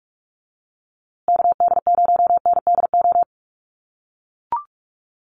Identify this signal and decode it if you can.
Morse code, heard as a keyed tone, about 44 wpm, 700 Hz. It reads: KD0NDO
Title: USA Call Signs Repeated 3x 45wpm